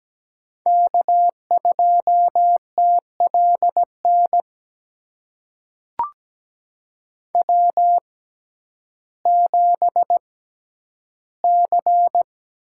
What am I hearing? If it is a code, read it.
Morse code, heard as a keyed tone, about 17 wpm, spaced as for 10 wpm, 700 Hz. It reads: K2TLN W7C
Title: USA Call Signs - 5 Characters - 6x Character Spacing - 17wpm